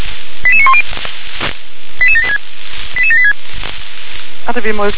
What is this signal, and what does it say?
Five-tone selective call (ZVEI), heard as quick runs of five tones, tones of about 70 ms, with speech. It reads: 88910 88786 88786